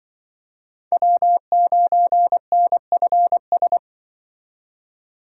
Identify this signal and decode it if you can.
Morse code, heard as a keyed tone, about 24 wpm, 700 Hz. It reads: W9NFS
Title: Speed Racing - USA 1x3 Call Signs 20wpm